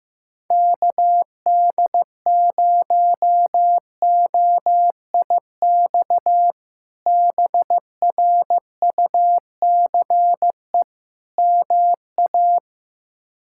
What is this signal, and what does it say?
Morse code, heard as a keyed tone, about 15 wpm, 700 Hz. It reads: KD0OIX BRUCE MA
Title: K1USN SST QSO Practice - 15wpm